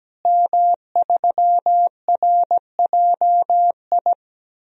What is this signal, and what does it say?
Morse code, heard as a keyed tone, about 17 wpm, 700 Hz. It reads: M3RJI
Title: DXCC-100 Call Signs - 5 Characters - Repeated 2x - 17wpm